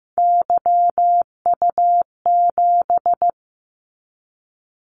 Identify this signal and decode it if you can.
Morse code, heard as a keyed tone, about 15 wpm, 700 Hz. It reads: YU7
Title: Callsign Prefixes for Top 25 DXCC - 15wpm